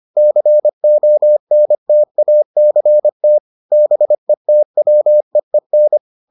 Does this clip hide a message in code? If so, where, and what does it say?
Yes: Morse code from 0.2 to 6.0 s, heard as a keyed tone, about 25 wpm, 600 Hz. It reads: CONTACT BETWEEN